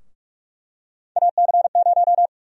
Morse code, heard as a keyed tone, about 44 wpm, 700 Hz. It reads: AC0
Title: USA Call Sign Prefixes - Repeated 2x - 45wpm